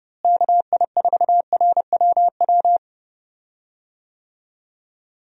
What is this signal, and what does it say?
Morse code, heard as a keyed tone, about 30 wpm, 700 Hz. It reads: KI4RWW
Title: USA Call Signs - 6 Characters - Repeated 2x - 30wpm